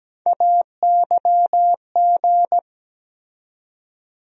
Morse code, heard as a keyed tone, about 17 wpm, 700 Hz. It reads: AYG
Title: USA Call Sign Postfixes - 3 Letters 17wpm